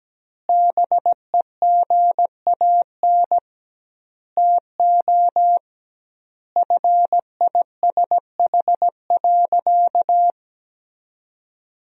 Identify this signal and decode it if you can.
Morse code, heard as a keyed tone, about 17 wpm, 700 Hz. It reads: BEGAN TO FISH.
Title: 2x Word-Spacing - Sentences from Top 300 Words - 17wpm